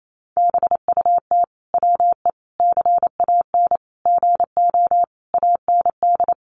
Morse code, heard as a keyed tone, about 28 wpm, 700 Hz. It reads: BUT WE CAN GO AND
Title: Sentences from Top 100 Words 28wpm